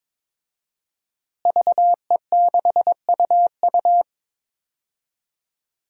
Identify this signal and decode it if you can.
Morse code, heard as a keyed tone, about 22 wpm, 700 Hz. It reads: VE6UU
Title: DXCC-100 Call Signs - 5 Characters - Repeated 2x - 22wpm